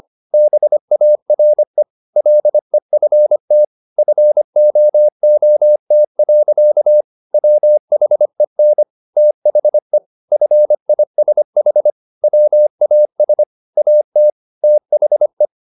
Morse code, heard as a keyed tone, about 25 wpm, 600 Hz. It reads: BARE LEFT FOOT. WHEN THE FISH WAS AT THE